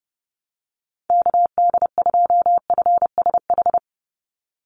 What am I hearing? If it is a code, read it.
Morse code, heard as a keyed tone, about 30 wpm, 700 Hz. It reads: KD2FSH
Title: Call Signs Repeated 3x - 30wpm